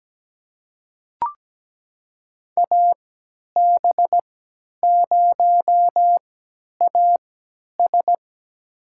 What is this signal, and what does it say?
Morse code, heard as a keyed tone, about 17 wpm, spaced as for 10 wpm, 700 Hz. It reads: AB0AS
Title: USA Call Signs - 5 Characters - 3x Character Spacing - 17wpm